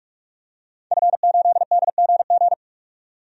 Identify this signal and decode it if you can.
Morse code, heard as a keyed tone, about 43 wpm, 700 Hz. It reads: F8DGG
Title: DXCC-100 Call Signs - 5 Characters - 45wpm